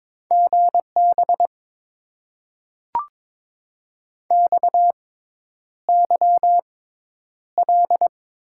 Morse code, heard as a keyed tone, about 22 wpm, spaced as for 10 wpm, 700 Hz. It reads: GB XYL
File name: Top 90 QSO Elements - 6x Character Spacing - 22wpm